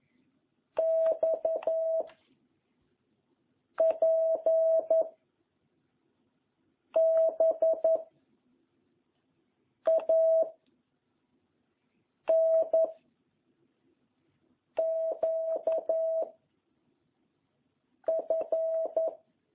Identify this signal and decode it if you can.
Morse code, heard as a keyed tone, about 11 wpm, 660 Hz. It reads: X P B A N Q F